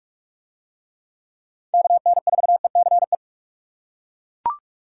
Morse code, heard as a keyed tone, about 41 wpm, 700 Hz. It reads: KN4ECE